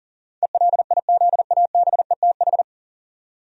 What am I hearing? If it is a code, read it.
Morse code, heard as a keyed tone, about 38 wpm, 700 Hz. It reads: ELIZABETH
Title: USA Common Names - 40wpm